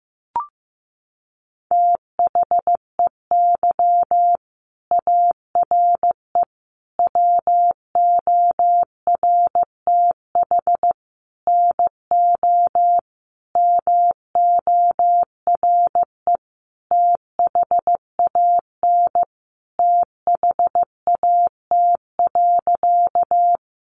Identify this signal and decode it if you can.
Morse code, heard as a keyed tone, about 15 wpm, 700 Hz. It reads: THEY ARE WORTH NO MORE THAN THAT.
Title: Sentences from Top 800 Words 15wpm